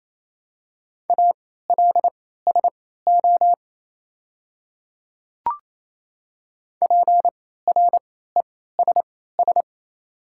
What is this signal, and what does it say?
Morse code, heard as a keyed tone, about 27 wpm, spaced as for 16 wpm, 690 Hz. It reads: ALSO PRESS
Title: 3x Character Spacing - Top 500 Words - 28wpm